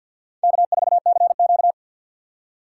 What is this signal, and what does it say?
Morse code, heard as a keyed tone, about 47 wpm, 700 Hz. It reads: K4CQ
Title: USA Call Signs - 4 Characters - 3x Character Spacing - 50wpm